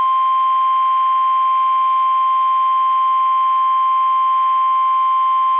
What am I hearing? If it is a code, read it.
Emergency Alert System (SAME) tones (the 1050 Hz attention tone). It is heard from 0.8 to 4.9 s.
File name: eas / KUSO / KUSO